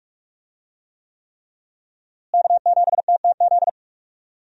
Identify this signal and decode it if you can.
Morse code, heard as a keyed tone, about 46 wpm, 700 Hz. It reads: K7TTZ
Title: USA Call Signs - 5 Characters - Repeated 3x - 45wpm